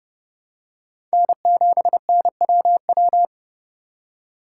Morse code, heard as a keyed tone, about 30 wpm, 700 Hz. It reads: N7NWW
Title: USA Call Signs - 5 Characters - 2x Character Spacing - 30wpm